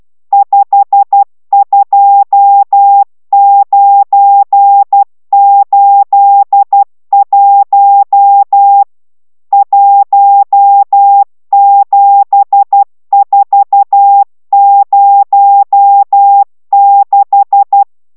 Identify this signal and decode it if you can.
Morse code, heard as a keyed tone, about 12 wpm, 800 Hz. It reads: 52981 17406